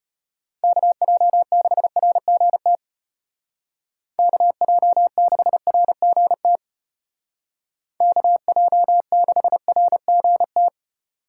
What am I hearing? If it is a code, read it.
Morse code, heard as a keyed tone, about 31 wpm, 700 Hz. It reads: KJ6RGT KJ6RGT KJ6RGT
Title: Speed Racing - USA 2x3 Call Signs 25wpm